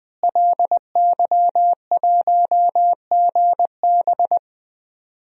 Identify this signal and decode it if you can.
Morse code, heard as a keyed tone, about 20 wpm, 700 Hz. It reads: LY1GB
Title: DXCC-100 Call Signs - 5 Characters - Repeated 3x - 20wpm